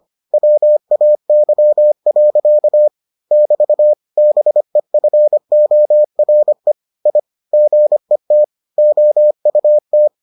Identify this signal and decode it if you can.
Morse code, heard as a keyed tone, about 25 wpm, 600 Hz. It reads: WAY. <BT> BEFORE I GET OUT